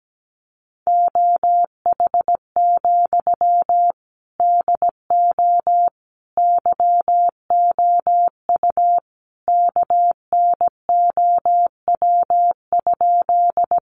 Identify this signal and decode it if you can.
Morse code, heard as a keyed tone, about 17 wpm, 700 Hz. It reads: OH, DO YOU KNOW?
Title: Sentences from Top 500 Words 17wpm